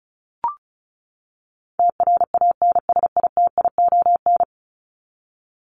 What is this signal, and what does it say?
Morse code, heard as a keyed tone, about 34 wpm, 700 Hz. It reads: TRANSITION
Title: Top 1751-2000 Words 35wpm